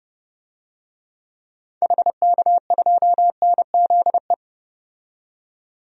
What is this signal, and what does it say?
Morse code, heard as a keyed tone, about 30 wpm, 700 Hz. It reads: HK2NZE
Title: DXCC-100 Call Signs - 6 Characters - 30wpm